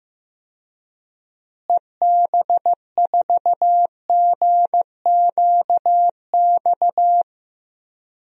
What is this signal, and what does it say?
Morse code, heard as a keyed tone, about 15 wpm, 700 Hz. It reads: EB4GQX
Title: DXCC-100 Call Signs - 6 Characters - Repeated 2x - 15wpm